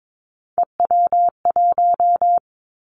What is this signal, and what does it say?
Morse code, heard as a keyed tone, about 22 wpm, 700 Hz. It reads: EW1
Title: Three Challenging Letter-Number Combinations 22wpm